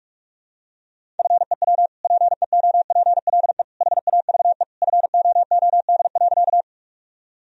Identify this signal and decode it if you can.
Morse code, heard as a keyed tone, about 46 wpm, 700 Hz. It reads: FEW PEOPLE HAVE FOOD.
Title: Sentences from Top 1000 Words - Review Part 1 - 45wpm